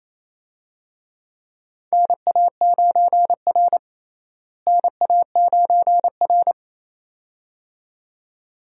Morse code, heard as a keyed tone, about 28 wpm, 700 Hz. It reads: NA9R NA9R
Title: USA Call Signs - 4 Characters - Repeated 2x - 28wpm